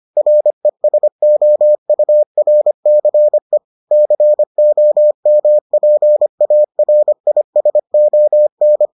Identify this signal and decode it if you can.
Morse code, heard as a keyed tone, about 25 wpm, 600 Hz. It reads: RESOURCE COMPARISON